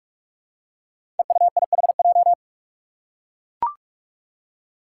Morse code, heard as a keyed tone, about 43 wpm, 700 Hz. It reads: EUIHJ
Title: Five Challenging Letter-Number Combinations 45wpm